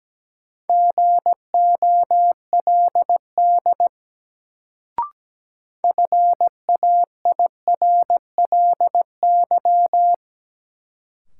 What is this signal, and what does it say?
Morse code, heard as a keyed tone, about 17 wpm, 700 Hz. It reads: GOLD FAIRLY
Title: Sentences from Top 1100 Words - New Word Drill Alt - 17wpm